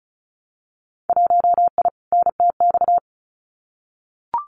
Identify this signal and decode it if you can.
Morse code, heard as a keyed tone, about 34 wpm, 700 Hz. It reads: 1I NTX
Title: Winter Field Day Exchanges Encore 35wpm